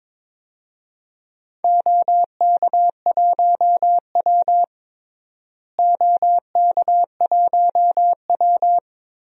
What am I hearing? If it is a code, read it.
Morse code, heard as a keyed tone, about 21 wpm, 700 Hz. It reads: OK1W OK1W
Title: DXCC-100 Call Signs - 4 Characters - Repeated 2x - 22wpm